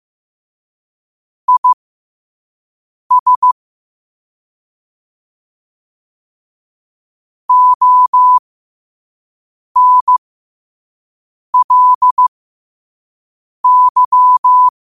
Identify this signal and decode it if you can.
Morse code, heard as a keyed tone, about 15 wpm, spaced as for 5 wpm, 1000 Hz. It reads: IS ONLY
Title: 5 WPM morse code quotes for Fri, 01 Mar 2024